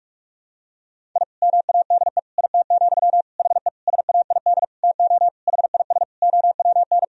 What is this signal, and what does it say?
Morse code, heard as a keyed tone, about 46 wpm, 700 Hz. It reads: I MADE IT, HE SAID TO HIS OWN
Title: Sentences from Top 1000 Words - Review Part 1 - 45wpm